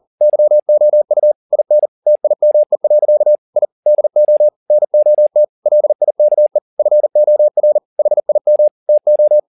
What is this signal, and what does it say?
Morse code, heard as a keyed tone, about 40 wpm, 600 Hz. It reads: YOU IN TIME. I DO NOT LIKE FOR HIM TO